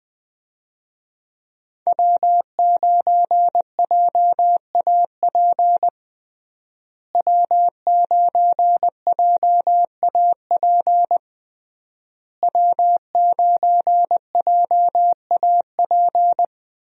Morse code, heard as a keyed tone, about 20 wpm, 700 Hz. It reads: W9JAP W9JAP W9JAP
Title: USA Call Signs - 5 Characters - Repeated 3x - 20wpm